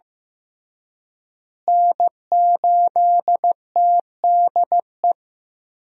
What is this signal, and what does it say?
Morse code, heard as a keyed tone, about 15 wpm, 700 Hz. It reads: N8TDE
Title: USA Call Signs - 5 Characters - Repeated 3x - 15wpm